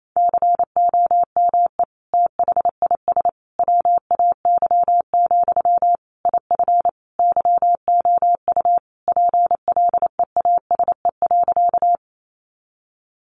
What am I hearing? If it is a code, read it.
Morse code, heard as a keyed tone, about 28 wpm, 700 Hz. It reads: COME THIS WAY, IF YOU PLEASE.
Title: Sentences from Top 1500 Words 28wpm